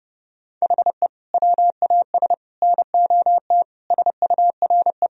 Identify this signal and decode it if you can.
Morse code, heard as a keyed tone, about 32 wpm, 700 Hz. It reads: HE WAS NOT SURE
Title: Sentences from Top 1000 Words - Review Part 1 - 30wpm